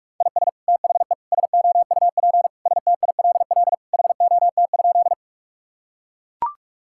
Morse code, heard as a keyed tone, about 42 wpm, 700 Hz. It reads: IS THE SOUP STILL HOT?
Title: Sentences from Top 1000 Words - Review Part 6 - 45wpm